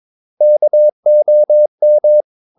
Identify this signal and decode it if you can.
Morse code, heard as a keyed tone, about 22 wpm, 600 Hz. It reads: KOM
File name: Ferie QRS 22wpm